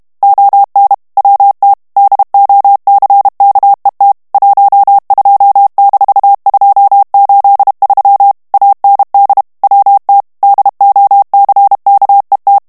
Morse code, heard as a keyed tone, about 32 wpm, 800 Hz. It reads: ON WT DOCKET 12-283 AND WT DOCKET
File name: morse mp3 / ANGLAIS texte clair 3 vitesse 30 mots minute